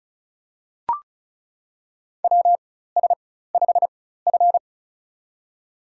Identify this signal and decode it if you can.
Morse code, heard as a keyed tone, about 34 wpm, spaced as for 16 wpm, 700 Hz. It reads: WS5F
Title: USA Call Signs - 4 Characters - 4x Character Spacing - 35wpm